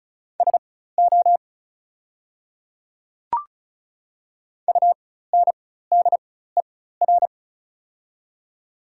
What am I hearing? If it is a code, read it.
Morse code, heard as a keyed tone, about 34 wpm, spaced as for 16 wpm, 700 Hz. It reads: SO UNDER
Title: 4x Character Spacing - Top 300 Words - 35wpm